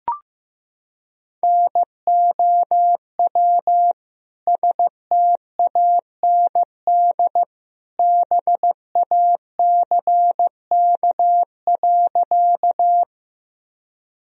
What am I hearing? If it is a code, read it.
Morse code, heard as a keyed tone, about 15 wpm, 700 Hz. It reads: NOW STAND BACK.
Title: Sentences from Top 1000 Words - Review Part 1 - 15wpm